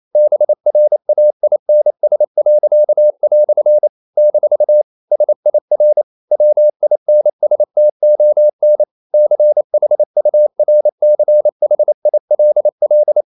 Morse code, heard as a keyed tone, about 28 wpm, 600 Hz. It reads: BRAINS." - SIR WINSTON CHURCHILL
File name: CW / Practice / EnglishQuotations0228